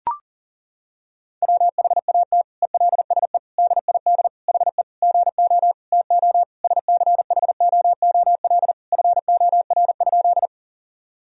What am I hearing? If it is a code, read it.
Morse code, heard as a keyed tone, about 40 wpm, 700 Hz. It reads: WHAT ELSE DID HE GO TO SCHOOL FOR?
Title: Sentences from Top 1000 Words - Review Part 1 - 40wpm